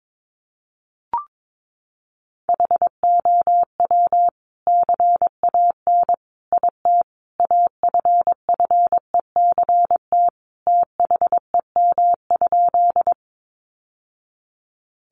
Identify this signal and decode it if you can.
Morse code, heard as a keyed tone, about 22 wpm, 700 Hz. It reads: HOW CAN IT AFFECT THEM?
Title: Sentences from Top 900 Words 22wpm